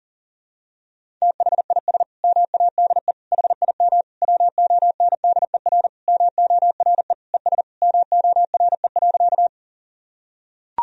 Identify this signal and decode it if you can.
Morse code, heard as a keyed tone, about 40 wpm, 700 Hz. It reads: THIS MADE HIM WONDER MORE ES MORE.
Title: Sentences from Top 500 Words and Abbreviations 40wpm